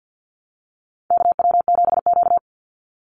Morse code, heard as a keyed tone, about 50 wpm, 700 Hz. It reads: KW7Q
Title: Call Signs Repeated 2x - 50wpm